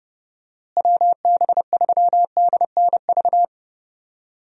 Morse code, heard as a keyed tone, about 32 wpm, 700 Hz. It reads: WB3DNV
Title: USA Call Signs - 6 Characters - Repeated 2x - 30wpm